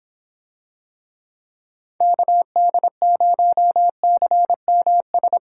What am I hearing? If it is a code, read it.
Morse code, heard as a keyed tone, about 26 wpm, 700 Hz. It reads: KD0CMS